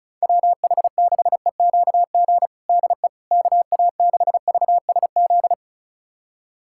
Morse code, heard as a keyed tone, about 34 wpm, 690 Hz. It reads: WH6EQG DE KA6VSZ
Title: USA Callsign DE Callsign - 6 Characters - 35wpm